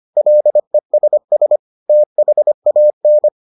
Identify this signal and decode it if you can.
Morse code, heard as a keyed tone, about 25 wpm, 600 Hz. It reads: LESS THAN